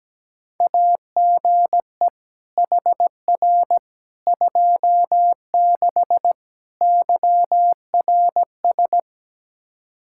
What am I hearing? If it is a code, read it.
Morse code, heard as a keyed tone, about 17 wpm, 700 Hz. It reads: AGE HR 26 YRS
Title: Practice QSO - Exchange 3 - More Info - 17wpm